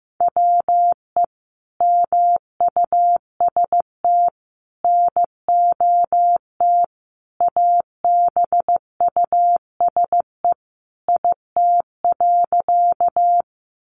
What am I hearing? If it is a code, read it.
Morse code, heard as a keyed tone, about 15 wpm, 700 Hz. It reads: WE MUST NOT ABUSE IT.
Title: Sentences from Top 1300 Words 15wpm